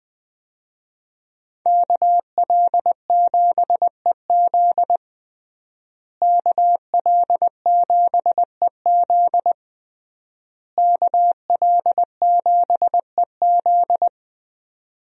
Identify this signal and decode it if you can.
Morse code, heard as a keyed tone, about 20 wpm, 700 Hz. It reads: KL7EZ KL7EZ KL7EZ
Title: DXCC-100 Call Signs - 5 Characters - Repeated 3x - 20wpm